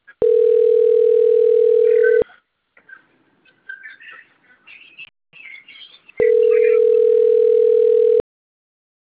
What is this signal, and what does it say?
Ringback tone (North America), 440 + 480 Hz.
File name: rg-1600-unknown- - - .62591